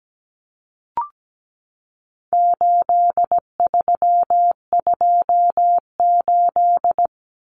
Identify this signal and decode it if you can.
Morse code, heard as a keyed tone, about 17 wpm, 700 Hz. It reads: 8328
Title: Numbers - 4-Digit 17wpm